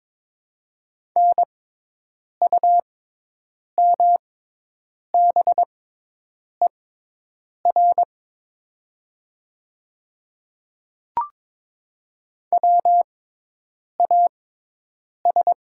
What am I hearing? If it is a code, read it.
Morse code, heard as a keyed tone, about 22 wpm, spaced as for 8 wpm, 700 Hz. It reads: NUMBER WAS
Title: 6x Character Spacing - Top 100 Words - 22wpm